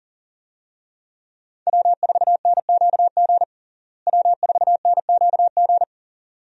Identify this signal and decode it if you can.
Morse code, heard as a keyed tone, about 40 wpm, 700 Hz. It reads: W4NQG W4NQG
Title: USA Call Signs - 5 Characters - Repeated 2x - 40wpm